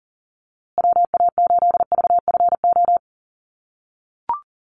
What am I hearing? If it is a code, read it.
Morse code, heard as a keyed tone, about 40 wpm, 700 Hz. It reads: WA8VFO
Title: USA Call Signs Repeated 3x 40wpm